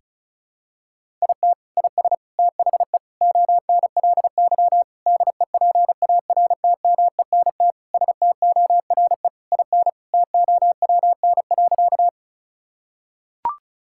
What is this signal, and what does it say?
Morse code, heard as a keyed tone, about 36 wpm, 700 Hz. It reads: IT IS THE ONLY DEPARTMENT STORE IN TOWN.
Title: Sentences from Top 1000 Words - Review Part 5 - 35wpm